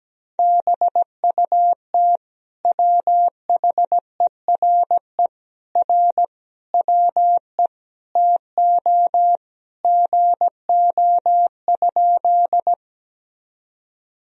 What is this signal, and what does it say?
Morse code, heard as a keyed tone, about 17 wpm, 700 Hz. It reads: BUT WHERE R WE TO GO?
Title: Sentences from Top 200 Words and Abbreviations 17wpm